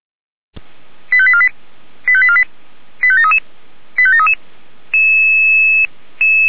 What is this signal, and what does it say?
Five-tone selective call (ZVEI), heard as quick runs of five tones, tones of about 70 ms. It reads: 85848 85848 85733 85733
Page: SDR/ZVEI